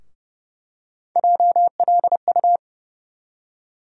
Morse code, heard as a keyed tone, about 31 wpm, 700 Hz. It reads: JLU
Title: USA Call Sign Postfixes - 3 Letters 30wpm